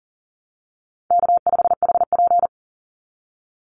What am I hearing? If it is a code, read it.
Morse code, heard as a keyed tone, about 40 wpm, 700 Hz. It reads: K5HP
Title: Call Signs Repeated 2x - 40wpm